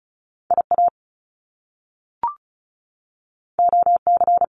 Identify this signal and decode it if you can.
Morse code, heard as a keyed tone, about 36 wpm, 700 Hz. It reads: IA OC
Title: Top Two Letter Combinations in Words 35wpm